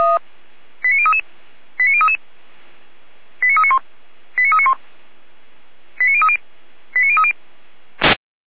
Five-tone selective call (ZVEI), heard as quick runs of five tones, tones of about 70 ms. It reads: 89033 89033 89391 89391 89030 89030